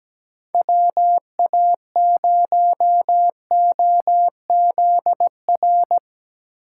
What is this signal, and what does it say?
Morse code, heard as a keyed tone, about 17 wpm, 700 Hz. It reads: WA0OZR
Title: USA Call Signs - 6 Characters - 6x Character Spacing - 17wpm